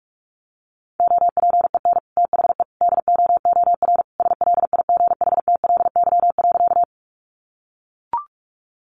Morse code, heard as a keyed tone, about 44 wpm, 700 Hz. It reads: OPEN THE DOOR SLIGHTLY.
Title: Sentences from Top 1300 Words 45wpm